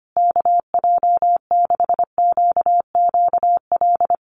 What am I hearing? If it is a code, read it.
Morse code, heard as a keyed tone, about 25 wpm, 700 Hz. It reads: KJ6QQL
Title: USA Call Signs Repeated 3x 25wpm